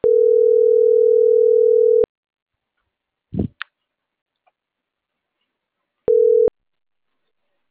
Ringback tone (North America), 440 + 480 Hz.